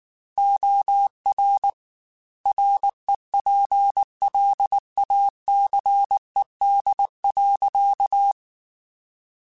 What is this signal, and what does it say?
Morse code, heard as a keyed tone, about 19 wpm, 800 Hz. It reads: OR REPLACED.